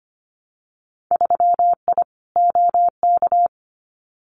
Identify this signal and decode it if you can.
Morse code, heard as a keyed tone, about 25 wpm, 700 Hz. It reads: 3I OK
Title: Winter Field Day Exchanges Encore 25wpm